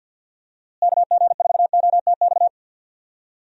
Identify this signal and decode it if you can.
Morse code, heard as a keyed tone, about 52 wpm, 700 Hz. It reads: KG4OTX
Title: USA Call Signs - 6 Characters - Repeated 3x - 50wpm